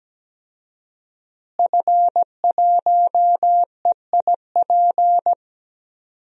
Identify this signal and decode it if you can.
Morse code, heard as a keyed tone, about 17 wpm, 700 Hz. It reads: F1EIP
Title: DXCC-100 Call Signs - 5 Characters - Repeated 3x - 17wpm